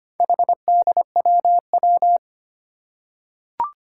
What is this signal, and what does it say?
Morse code, heard as a keyed tone, about 25 wpm, 700 Hz. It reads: HDWW
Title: Four Challenging Letter-Number Combinations 25wpm